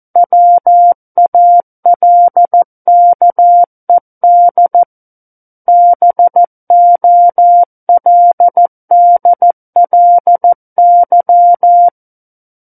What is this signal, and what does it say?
Morse code, heard as a keyed tone, about 14 wpm, 700 Hz. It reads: WALKED BOLDLY
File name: Book One / 05-Chapter 5 - 14 WPM